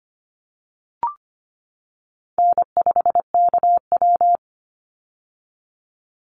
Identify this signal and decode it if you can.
Morse code, heard as a keyed tone, about 25 wpm, 700 Hz. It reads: N5KW
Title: Call Signs - 25wpm